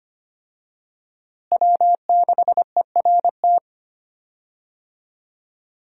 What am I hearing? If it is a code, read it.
Morse code, heard as a keyed tone, about 25 wpm, 700 Hz. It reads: W6ERT
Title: USA Call Signs - 5 Characters - Repeated 2x - 25wpm